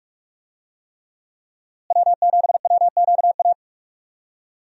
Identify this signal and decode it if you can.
Morse code, heard as a keyed tone, about 45 wpm, 700 Hz. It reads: W7WQA
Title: USA Call Signs - 5 Characters - Repeated 3x - 45wpm